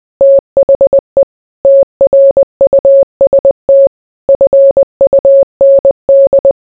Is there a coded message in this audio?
Yes: Morse code from 0.2 to 6.5 s, heard as a keyed tone, about 20 wpm, 550 Hz. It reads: THE TRUST FUND